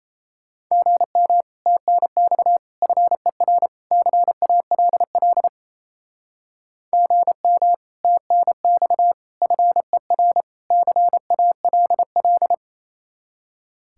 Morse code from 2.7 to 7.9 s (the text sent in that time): FER CALL GM